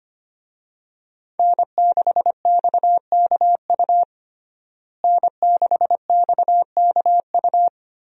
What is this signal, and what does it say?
Morse code, heard as a keyed tone, about 25 wpm, 700 Hz. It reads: N6XKU N6XKU
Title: USA Call Signs - 5 Characters - Repeated 3x - 25wpm